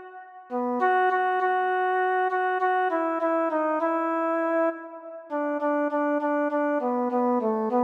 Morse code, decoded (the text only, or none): none